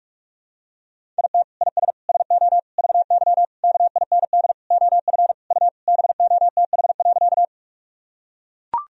Morse code, heard as a keyed tone, about 41 wpm, 700 Hz. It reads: IT IS SO VY KIND OF U BOTH.
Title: Sentences from Top 300 Words and Abbreviations 45wpm